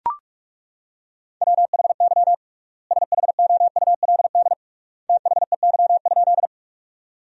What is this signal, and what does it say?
Morse code, heard as a keyed tone, about 41 wpm, 700 Hz. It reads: WHY SHOULD THEY?
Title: 3x Word-Spacing - Sentences from Top 300 Words - 45wpm